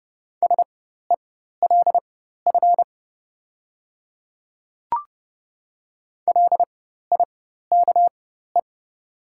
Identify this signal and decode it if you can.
Morse code, heard as a keyed tone, about 29 wpm, spaced as for 14 wpm, 700 Hz. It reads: SELF LIKE
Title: 4x Character Spacing - Top 300 Words - 30wpm